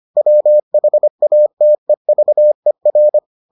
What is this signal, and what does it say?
Morse code, heard as a keyed tone, about 25 wpm, 600 Hz. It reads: WHATEVER